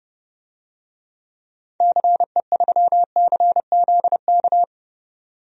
Morse code, heard as a keyed tone, about 32 wpm, 700 Hz. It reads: CE3CZK
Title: DXCC-100 Call Signs - 6 Characters - 30wpm